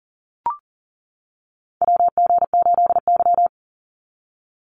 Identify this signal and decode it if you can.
Morse code, heard as a keyed tone, about 40 wpm, 700 Hz. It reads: WG8Y
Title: NA SOTA Chaser Callsigns 40wpm